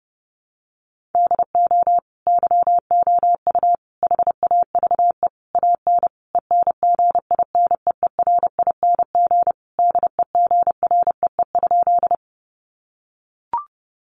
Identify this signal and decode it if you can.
Morse code, heard as a keyed tone, about 30 wpm, 700 Hz. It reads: DO YOU HAVE AN ENGINEERING DEGREE?
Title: Sentences from Top 1500 Words 30wpm